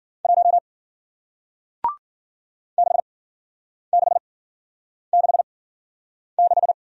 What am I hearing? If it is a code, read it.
Morse code, heard as a keyed tone, about 53 wpm, spaced as for 9 wpm, 700 Hz. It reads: 1 6666